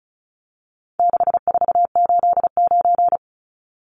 Morse code, heard as a keyed tone, about 34 wpm, 700 Hz. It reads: 6489
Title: Numbers - 4-Digit 35wpm